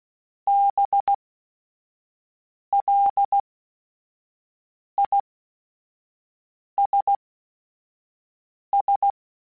Morse code, heard as a keyed tone, about 16 wpm, spaced as for 5 wpm, 800 Hz. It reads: BLISS